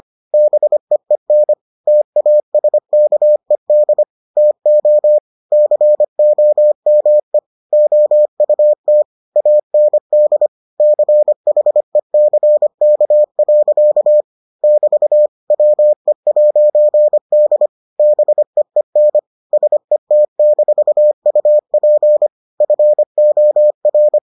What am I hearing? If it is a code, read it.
Morse code, heard as a keyed tone, about 25 wpm, 600 Hz. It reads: BEEN TASKED TO COME OUT AND CHECK. <BT> WE'D BEEN SET-UP FOR